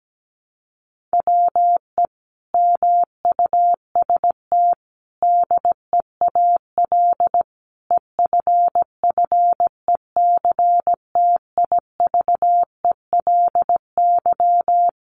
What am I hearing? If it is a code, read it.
Morse code, heard as a keyed tone, about 17 wpm, 700 Hz. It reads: WE MUST DEAL EFFECTIVELY
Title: Sentences from Top 1400 Words 17wpm